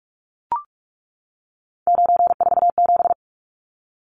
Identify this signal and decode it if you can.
Morse code, heard as a keyed tone, about 44 wpm, 700 Hz. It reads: 947
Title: Numbers - 3-Digit 45wpm